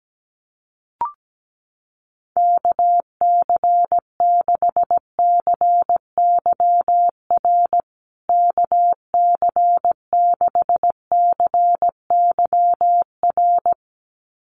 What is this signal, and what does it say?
Morse code, heard as a keyed tone, about 17 wpm, 700 Hz. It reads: KC6CYR KC6CYR
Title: USA Call Signs Repeated 2x 17wpm